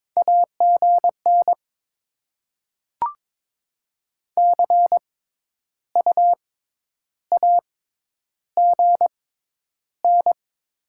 Morse code, heard as a keyed tone, about 22 wpm, spaced as for 10 wpm, 700 Hz. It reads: AGN CU AGN
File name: Top 20 QSO Elements - 6x Character Spacing - 22wpm